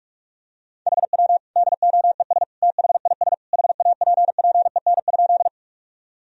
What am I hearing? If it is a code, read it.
Morse code, heard as a keyed tone, about 46 wpm, 700 Hz. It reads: HW DOES THIS HAPPEN?
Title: Sentences from Top 400 Words and Abbreviations 45wpm